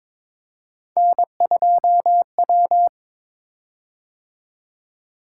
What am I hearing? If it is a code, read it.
Morse code, heard as a keyed tone, about 22 wpm, 700 Hz. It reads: N2W
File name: practice / 22wpm / Three Letter-Number Combinations - 22wpm